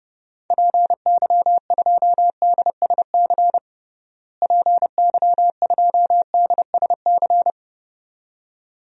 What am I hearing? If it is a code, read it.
Morse code, heard as a keyed tone, about 30 wpm, 700 Hz. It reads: PY2DSC PY2DSC
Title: DXCC-100 Call Signs - 6 Characters - Repeated 2x - 30wpm